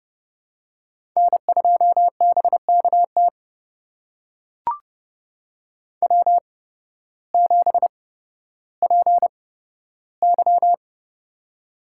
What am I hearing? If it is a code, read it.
Morse code, heard as a keyed tone, about 30 wpm, spaced as for 14 wpm, 700 Hz. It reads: N2BKT W7PY
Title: USA Call Signs - 8x Character Spacing - 30wpm